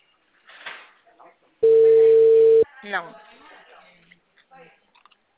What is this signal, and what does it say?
Ringback tone (China, 450 Hz), with speech.
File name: out- -2015- - - .73625